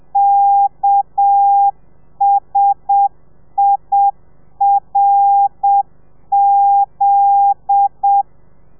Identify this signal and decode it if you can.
Morse code, heard as a keyed tone, about 7 wpm, 800 Hz. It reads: KSIRZ